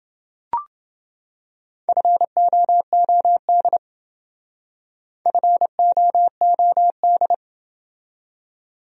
Morse code, heard as a keyed tone, about 29 wpm, 700 Hz. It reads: FOOD FOOD